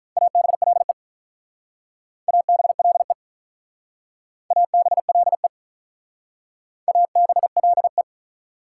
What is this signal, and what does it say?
Morse code, heard as a keyed tone, about 47 wpm, 700 Hz. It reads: ABLE ABLE ABLE ABLE